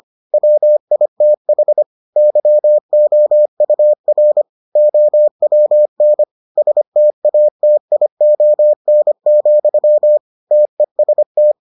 Morse code, heard as a keyed tone, about 25 wpm, 600 Hz. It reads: WITH YOUR OWN STATION, TEST